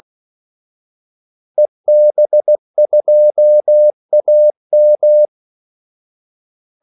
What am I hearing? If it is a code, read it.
Morse code, heard as a keyed tone, about 16 wpm, 600 Hz. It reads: EB2AM
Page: Kaldesignaler 16 wpm